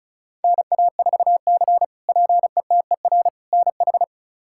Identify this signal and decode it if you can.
Morse code, heard as a keyed tone, about 36 wpm, 700 Hz. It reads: NA4C PETER NH
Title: K1USN SST QSO Practice - 35wpm